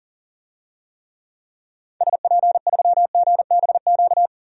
Morse code, heard as a keyed tone, about 37 wpm, 700 Hz. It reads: SP3GBQ
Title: DXCC-100 Call Signs - 6 Characters - 40wpm